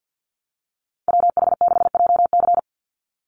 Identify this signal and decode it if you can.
Morse code, heard as a keyed tone, about 48 wpm, 700 Hz. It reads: WH6JC
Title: USA Call Signs 50wpm